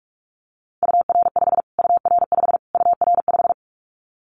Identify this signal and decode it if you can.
Morse code, heard as a keyed tone, about 44 wpm, 700 Hz. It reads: UR5 UR5 UR5
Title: Callsign Prefixes for Top 25 DXCC - Repeated 3x - 45wpm